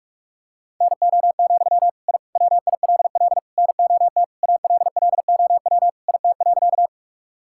Morse code, heard as a keyed tone, about 46 wpm, 700 Hz. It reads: NO, I WILL NOT ALLOW IT.
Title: Sentences from Top 1000 Words - Review Part 5 - 45wpm